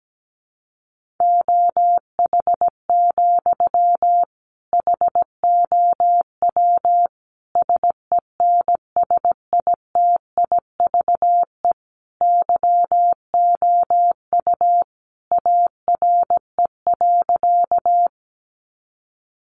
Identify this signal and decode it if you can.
Morse code, heard as a keyed tone, about 17 wpm, 700 Hz. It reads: OH, HOW SENSITIVE YOU ARE.
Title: Sentences from Top 1400 Words 17wpm